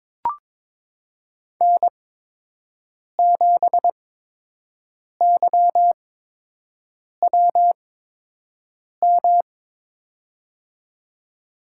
Morse code, heard as a keyed tone, about 22 wpm, spaced as for 6 wpm, 700 Hz. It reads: N7YWM
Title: USA Call Signs - 8x Character Spacing - 22wpm